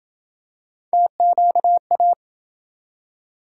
Morse code, heard as a keyed tone, about 30 wpm, 700 Hz. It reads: TQA